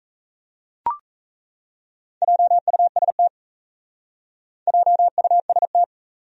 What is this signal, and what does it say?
Morse code, heard as a keyed tone, about 35 wpm, 700 Hz. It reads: JUST JUST